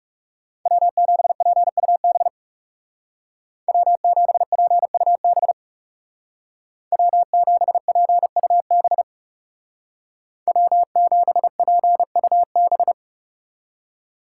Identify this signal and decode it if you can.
Morse code, heard as a keyed tone, about 37 wpm, 700 Hz. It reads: W7PUB W7PUB W7PUB W7PUB
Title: Speed Racing - USA 1x3 Call Signs 30wpm